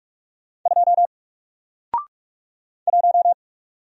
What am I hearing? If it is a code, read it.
Morse code, heard as a keyed tone, about 46 wpm, 700 Hz. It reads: 2 1
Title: Single Character (Letter:Number:Punctuation:Prosign) Alt 45wpm